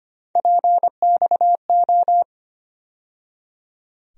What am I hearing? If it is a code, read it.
Morse code, heard as a keyed tone, about 25 wpm, 700 Hz. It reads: PXO
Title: USA Call Sign Postfixes - 3 Letters 25wpm